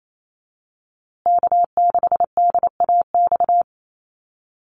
Morse code, heard as a keyed tone, about 28 wpm, 700 Hz. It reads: K6DAX
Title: USA Call Signs Repeated 3x 28wpm